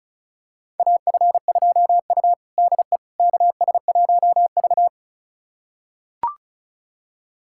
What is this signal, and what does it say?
Morse code, heard as a keyed tone, about 34 wpm, 700 Hz. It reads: AF2U DE KS1V
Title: USA Callsign DE Callsign - 4 Characters - 35wpm